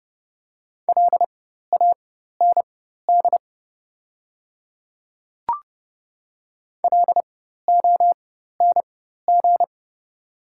Morse code, heard as a keyed tone, about 30 wpm, spaced as for 14 wpm, 700 Hz. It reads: LAND LONG
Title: 4x Character Spacing - Top 500 Words - 30wpm